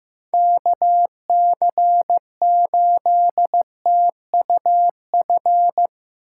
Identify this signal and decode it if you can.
Morse code, heard as a keyed tone, about 18 wpm, 700 Hz. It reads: KC8TUF
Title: Speed Racing - USA 2x3 Call Signs 15wpm